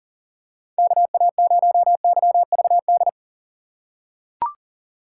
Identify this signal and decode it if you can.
Morse code, heard as a keyed tone, about 39 wpm, 700 Hz. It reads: KA0YVD
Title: USA Call Signs - 3x Character Spacing - 40wpm